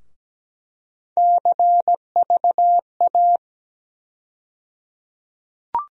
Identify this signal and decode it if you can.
Morse code, heard as a keyed tone, about 17 wpm, 700 Hz. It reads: CVA
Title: USA Call Sign Postfixes - 3 Letters 17wpm